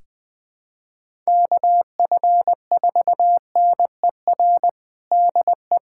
Morse code, heard as a keyed tone, about 21 wpm, 700 Hz. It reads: KF4NER DE
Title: USA Callsign DE Callsign 20wpm